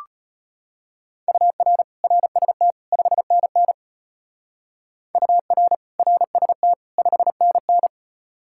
Morse code, heard as a keyed tone, about 31 wpm, 700 Hz. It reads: UR RST 5NN UR RST 5NN
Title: Speed-Racing Common QSO Phrases - 25wpm